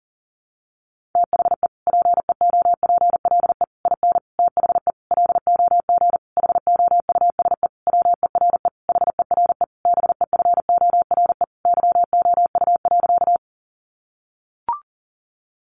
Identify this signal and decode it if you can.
Morse code, heard as a keyed tone, about 40 wpm, 700 Hz. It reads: THE PEOPLE IN THE LOG HOUSE WERE HERE BEFORE YOU.
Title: Sentences from Top 1400 Words 40wpm